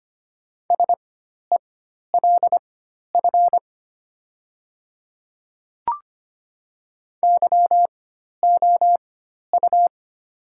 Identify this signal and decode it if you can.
Morse code, heard as a keyed tone, about 25 wpm, spaced as for 12 wpm, 700 Hz. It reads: SELF YOU
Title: 4x Character Spacing - Top 300 Words - 25wpm